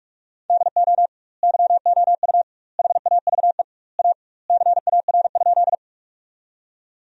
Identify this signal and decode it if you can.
Morse code, heard as a keyed tone, about 46 wpm, 700 Hz. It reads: DO YOU HAVE A CAR?
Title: 2x Word-Spacing - Sentences from Top 300 Words - 45wpm